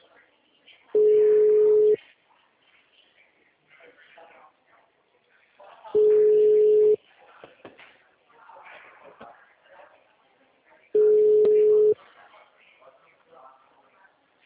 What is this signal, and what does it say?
Ringback tone (Europe, 425 Hz), with speech.